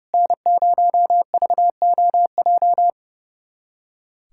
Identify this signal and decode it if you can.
Morse code, heard as a keyed tone, about 31 wpm, 700 Hz. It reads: N0VOJ
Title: USA Callsign DE Callsign - 5 Characters - 30wpm